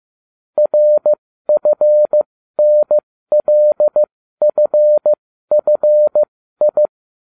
Morse code, heard as a keyed tone, about 15 wpm, spaced as for 12 wpm, 600 Hz. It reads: RFNLFFI